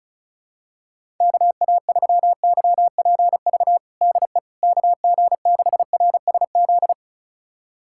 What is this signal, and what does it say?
Morse code, heard as a keyed tone, about 36 wpm, 700 Hz. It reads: KA3YPV DE KG6RSZ
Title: USA Callsign DE Callsign 35wpm